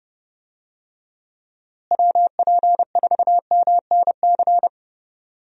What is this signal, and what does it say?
Morse code, heard as a keyed tone, about 30 wpm, 700 Hz. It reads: WP4MNC
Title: USA Call Signs - 6 Characters - Repeated 2x - 30wpm